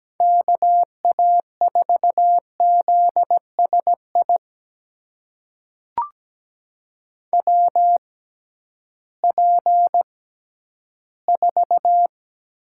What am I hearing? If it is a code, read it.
Morse code, heard as a keyed tone, about 17 wpm, spaced as for 11 wpm, 700 Hz. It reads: KA4ZSI WP4
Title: USA Call Signs - 6 Characters - 6x Character Spacing - 17wpm